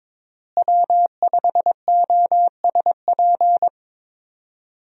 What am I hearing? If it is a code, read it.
Morse code, heard as a keyed tone, about 22 wpm, 700 Hz. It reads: W5OSP
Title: USA Call Signs - 5 Characters - Repeated 3x - 22wpm